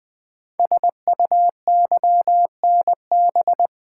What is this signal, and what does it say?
Morse code, heard as a keyed tone, about 20 wpm, 700 Hz. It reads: SUYNB
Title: Code Groups - 5 Letters - 20wpm